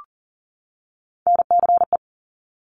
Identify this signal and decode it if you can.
Morse code, heard as a keyed tone, about 42 wpm, 700 Hz. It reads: NCE
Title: Top Three Letter Combinations in Words 40wpm